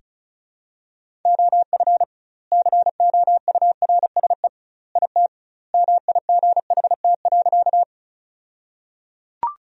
Morse code, heard as a keyed tone, about 34 wpm, 700 Hz. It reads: OF COURSE IT MIGHT.
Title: 2x Word-Spacing - Sentences from Top 500 Words - 35wpm